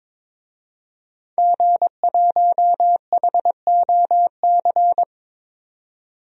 Morse code, heard as a keyed tone, about 22 wpm, 700 Hz. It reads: G1HOC
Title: DXCC-100 Call Signs - 5 Characters - 22wpm